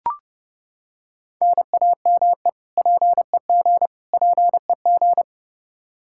Morse code, heard as a keyed tone, about 32 wpm, 700 Hz. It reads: NAME PEG PEG
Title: Practice QSO - Exchange 1 - Beginning - 30wpm